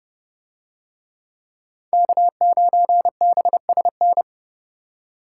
Morse code, heard as a keyed tone, about 28 wpm, 700 Hz. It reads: K9BSN